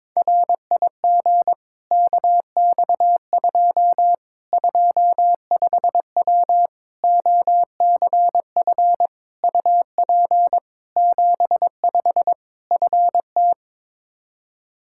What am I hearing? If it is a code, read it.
Morse code, heard as a keyed tone, about 22 wpm, 700 Hz. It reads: RIG KX2 25W OCF UP 75 FT
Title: K1USN SST QSO Practice 22wpm